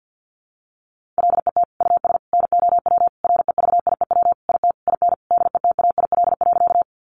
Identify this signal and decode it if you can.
Morse code, heard as a keyed tone, about 50 wpm, 700 Hz. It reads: LET US NOW REVIEW IT IN DETAIL.
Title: Sentences from Top 1500 Words 50wpm